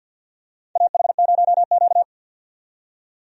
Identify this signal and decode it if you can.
Morse code, heard as a keyed tone, about 50 wpm, 700 Hz. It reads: AH0Q
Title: USA Call Signs - 4 Characters - 50wpm